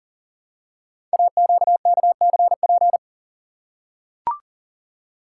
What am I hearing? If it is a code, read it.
Morse code, heard as a keyed tone, about 39 wpm, 700 Hz. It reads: AQKCP
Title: Code Groups - 5 Letters - 40wpm